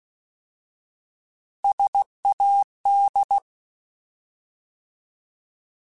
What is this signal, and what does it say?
Morse code, heard as a keyed tone, about 16 wpm, 800 Hz. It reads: SAD